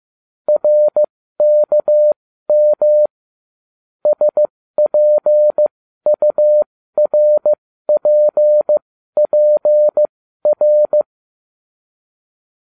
Morse code, heard as a keyed tone, about 15 wpm, spaced as for 12 wpm, 600 Hz. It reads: RKM SPURPPR